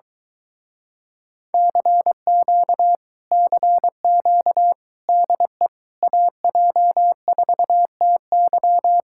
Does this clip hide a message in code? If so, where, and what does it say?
Yes: Morse code from 1.5 to 9.0 s, heard as a keyed tone, about 23 wpm, 700 Hz. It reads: CQ CQ DE AJ4TY